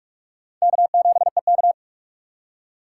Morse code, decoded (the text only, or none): K7EK